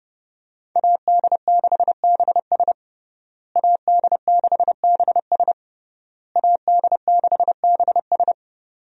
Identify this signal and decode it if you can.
Morse code, heard as a keyed tone, about 30 wpm, 700 Hz. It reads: AD6BS AD6BS AD6BS
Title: USA Call Signs - 5 Characters - Repeated 3x - 30wpm